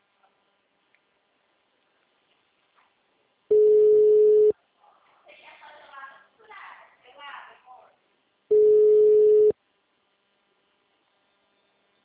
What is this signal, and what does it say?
Ringback tone (Europe, 425 Hz), with speech.